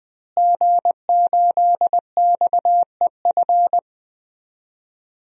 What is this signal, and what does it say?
Morse code, heard as a keyed tone, about 20 wpm, 700 Hz. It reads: G8XEF
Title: DXCC-100 Call Signs - 5 Characters - Repeated 2x - 20wpm